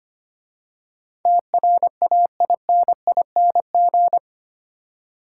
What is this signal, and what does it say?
Morse code, heard as a keyed tone, about 25 wpm, 700 Hz. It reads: TRAINING
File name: 8 Letter Words - First 3 Letters Given - 25wpm